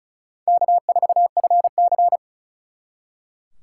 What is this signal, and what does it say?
Morse code, heard as a keyed tone, about 45 wpm, 700 Hz. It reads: K4FC